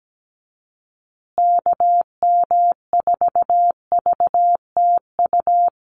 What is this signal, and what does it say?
Morse code, heard as a keyed tone, about 17 wpm, 700 Hz. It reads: KM4VTU
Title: USA Call Signs Repeated 2x 17wpm